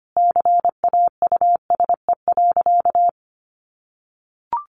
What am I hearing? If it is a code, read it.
Morse code, heard as a keyed tone, about 25 wpm, 700 Hz. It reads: CAUSE.
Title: Sentences from Top 1400 Words 25wpm